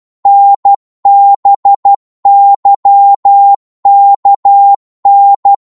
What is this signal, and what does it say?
Morse code, heard as a keyed tone, about 12 wpm, 800 Hz. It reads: NBYKN
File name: Morse / ltr-12wpm